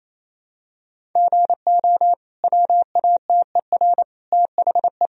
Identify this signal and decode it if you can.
Morse code, heard as a keyed tone, about 28 wpm, 700 Hz. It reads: GO WATER THE
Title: Sentences from Top 1000 Words - Review Part 1 - 28wpm